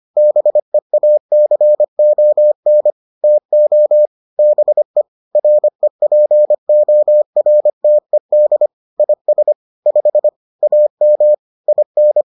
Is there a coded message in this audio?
Yes: Morse code from 0.2 to 12.2 s, heard as a keyed tone, about 25 wpm, 600 Hz. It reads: BEACON TO BE REPORTED IS 5 AM IN